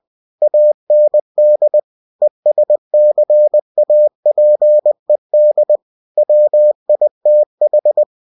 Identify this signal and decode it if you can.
Morse code, heard as a keyed tone, about 20 wpm, 600 Hz. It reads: AND ESCAPED WITH
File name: cw-books / winnie-the-pooh / winnie-20wpm-page-0094